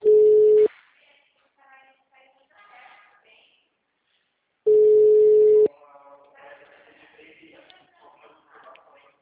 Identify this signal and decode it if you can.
Ringback tone (Europe, 425 Hz), with speech.
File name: out- -2004- - - .5438